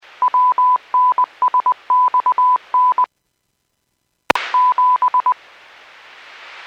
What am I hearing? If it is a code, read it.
Morse code, heard as a keyed tone, about 20 wpm, 1000 Hz. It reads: WNSXN 7